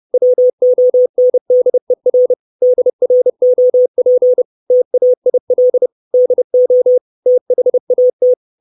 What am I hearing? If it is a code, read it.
Morse code, heard as a keyed tone, about 30 wpm, 500 Hz. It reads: WONDER DROP TAIL DO THAT
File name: 632 Words 30WPM